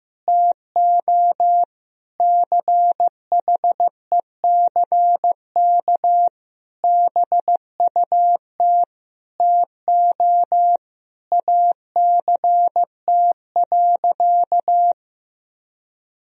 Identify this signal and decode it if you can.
Morse code, heard as a keyed tone, about 15 wpm, 700 Hz. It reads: TO CHECK BUT TO ACT.
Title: Sentences from Top 1000 Words - Review Part 3 - 15wpm